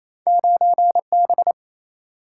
Morse code, heard as a keyed tone, about 28 wpm, 700 Hz. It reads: 9 B
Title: Single-Letter Single-Number 28wpm